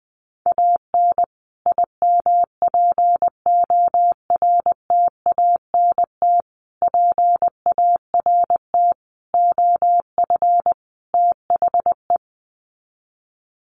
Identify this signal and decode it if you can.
Morse code, heard as a keyed tone, about 19 wpm, 700 Hz. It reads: AN IMPORTANT PART OF THE
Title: Sets of 5 Words Encore 20wpm